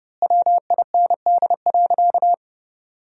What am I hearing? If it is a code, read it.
Morse code, heard as a keyed tone, about 30 wpm, 700 Hz. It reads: WIND.
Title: 2x Word-Spacing - Sentences from Top 400 Words - 30wpm